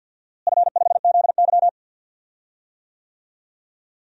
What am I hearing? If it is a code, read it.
Morse code, heard as a keyed tone, about 53 wpm, 700 Hz. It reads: F5ZY
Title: DXCC-100 Call Signs - 4 Characters - Repeated 2x - 50wpm